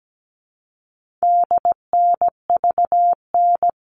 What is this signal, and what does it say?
Morse code, heard as a keyed tone, about 17 wpm, 700 Hz. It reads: DNVN
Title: All Cut Numbers - 4-Digit 17wpm